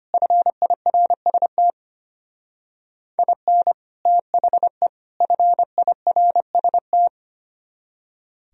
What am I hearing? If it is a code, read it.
Morse code, heard as a keyed tone, about 32 wpm, 700 Hz. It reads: FIRST IN THE FIRST